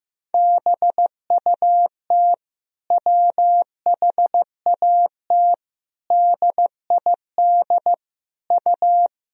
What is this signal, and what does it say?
Morse code, heard as a keyed tone, about 15 wpm, 700 Hz. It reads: BUT WHAT DID U
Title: Sentences from Top 100 Words and Abbreviations 15wpm